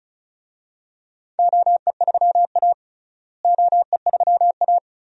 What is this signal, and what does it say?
Morse code, heard as a keyed tone, about 36 wpm, 700 Hz. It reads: OE3A OE3A
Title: DXCC-100 Call Signs - 4 Characters - Repeated 2x - 35wpm